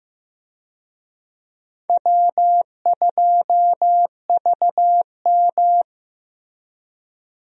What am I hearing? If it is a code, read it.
Morse code, heard as a keyed tone, about 15 wpm, 700 Hz. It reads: W2VM
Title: USA Call Signs - 4 Characters - Repeated 3x - 15wpm